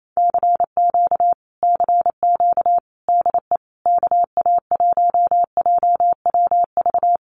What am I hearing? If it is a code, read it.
Morse code, heard as a keyed tone, about 28 wpm, 700 Hz. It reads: CQ CQ DE KA1JWV
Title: Common QSO Phrases Encore 28wpm